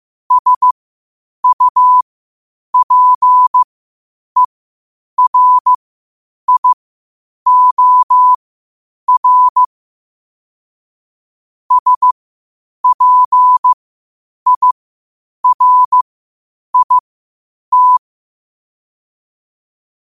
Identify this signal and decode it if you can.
Morse code, heard as a keyed tone, about 15 wpm, spaced as for 8 wpm, 1000 Hz. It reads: SUPERIOR SPIRIT
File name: qotdfiles / 8wpm